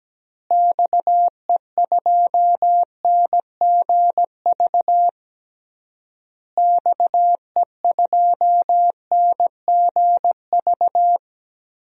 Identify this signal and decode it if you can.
Morse code, heard as a keyed tone, about 17 wpm, 700 Hz. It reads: XE2NGV XE2NGV
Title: DXCC-100 Call Signs - 6 Characters - Repeated 3x - 17wpm